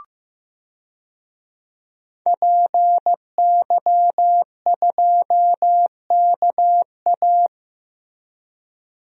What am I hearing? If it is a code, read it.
Morse code, heard as a keyed tone, about 15 wpm, 700 Hz. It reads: PY2KA
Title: DXCC-100 Call Signs - 5 Characters - Repeated 2x - 15wpm